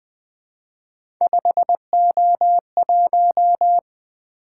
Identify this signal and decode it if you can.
Morse code, heard as a keyed tone, about 20 wpm, 700 Hz. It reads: 5O1
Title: Three Letter-Number Combinations - 20wpm